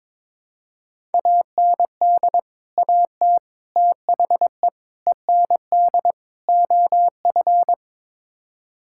Morse code, heard as a keyed tone, about 21 wpm, 700 Hz. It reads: AND AT THE END OF
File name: practice / 22wpm / Build A Phrase - 5-Word Phrases - 22wpm